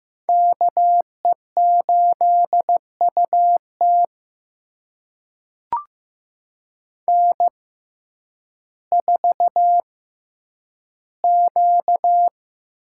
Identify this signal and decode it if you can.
Morse code, heard as a keyed tone, about 15 wpm, spaced as for 9 wpm, 700 Hz. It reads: KE8UT N4Q
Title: USA Call Signs - 5 Characters - 6x Character Spacing - 15wpm